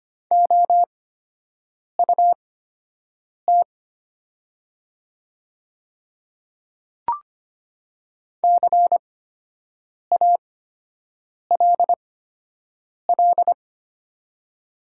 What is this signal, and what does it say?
Morse code, heard as a keyed tone, about 25 wpm, spaced as for 7 wpm, 700 Hz. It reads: OUT CALL
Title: 8x Character Spacing - Top 100 Words - 25wpm